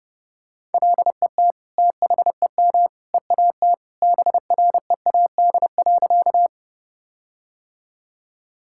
Morse code, heard as a keyed tone, about 30 wpm, 700 Hz. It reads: LET THEM EAT BREAD.
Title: Sentences from Top 1600 Words 30wpm